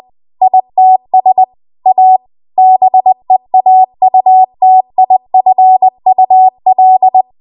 Morse code, heard as a keyed tone, about 20 wpm, 750 Hz. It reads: ITS A BEAUTIFUL
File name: Morse / 20WPM